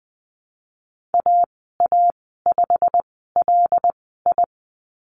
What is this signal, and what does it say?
Morse code, heard as a keyed tone, about 20 wpm, spaced as for 15 wpm, 700 Hz. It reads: AA5LI
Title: 161 Lesson 26g - Letter g - Call Signs 20wpm